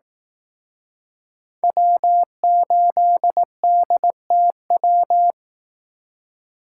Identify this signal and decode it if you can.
Morse code, heard as a keyed tone, about 18 wpm, 700 Hz. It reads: W8DTW
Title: Speed Racing - USA 1x3 Call Signs 15wpm